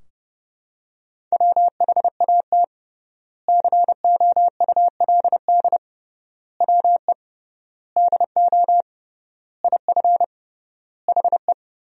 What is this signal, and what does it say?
Morse code, heard as a keyed tone, about 30 wpm, 700 Hz. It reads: WHAT COULD WE DO IF HE
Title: 3x Word-Spacing - Sentences from Top 400 Words - 30wpm